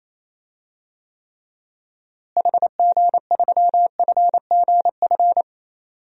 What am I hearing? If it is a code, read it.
Morse code, heard as a keyed tone, about 27 wpm, 700 Hz. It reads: HG3FGF
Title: DXCC-100 Call Signs - 6 Characters - Repeated 3x - 28wpm